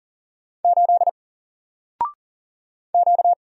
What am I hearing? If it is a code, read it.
Morse code, heard as a keyed tone, about 40 wpm, 700 Hz. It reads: 8 Q
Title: Single Letter-Number Alt 40wpm